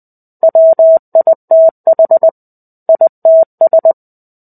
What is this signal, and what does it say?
Morse code, heard as a keyed tone, about 20 wpm, 650 Hz. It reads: WITH ITS
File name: 05-Chapter-05 20 WPM